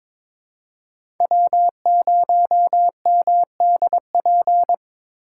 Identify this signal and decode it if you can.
Morse code, heard as a keyed tone, about 21 wpm, 700 Hz. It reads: W0MDP
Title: USA Call Signs - 5 Characters - 3x Character Spacing - 22wpm